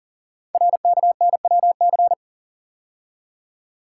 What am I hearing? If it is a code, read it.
Morse code, heard as a keyed tone, about 42 wpm, 700 Hz. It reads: RKNWC
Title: Code Groups - 5 Letters - 40wpm